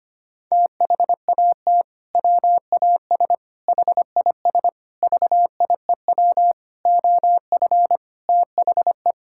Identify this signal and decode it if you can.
Morse code, heard as a keyed tone, about 25 wpm, 700 Hz. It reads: THAT WAS HIS VIEW OF THE
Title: Sentences from Top 1000 Words - Review Part 6 - 25wpm